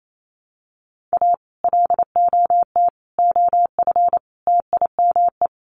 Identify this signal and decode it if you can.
Morse code, heard as a keyed tone, about 28 wpm, 700 Hz. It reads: A LOT OF TIME
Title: Sets of 4 Words 28wpm